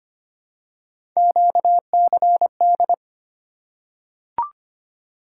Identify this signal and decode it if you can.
Morse code, heard as a keyed tone, about 25 wpm, 700 Hz. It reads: QCD
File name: Three Letter-Number Combinations - 25wpm